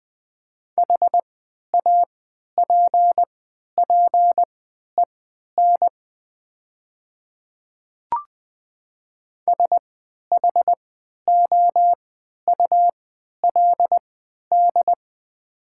Morse code, heard as a keyed tone, about 20 wpm, spaced as for 11 wpm, 700 Hz. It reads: HAPPEN SHOULD
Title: 3x Character Spacing - Top 500 Words - 20wpm